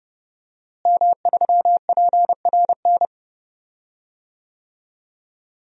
Morse code, heard as a keyed tone, about 32 wpm, 700 Hz. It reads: M3PRN
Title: DXCC-100 Call Signs - 5 Characters - Repeated 2x - 30wpm